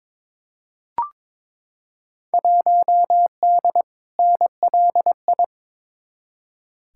Morse code, heard as a keyed tone, about 21 wpm, 700 Hz. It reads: 1D NLI
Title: Field Day Exchanges - 22wpm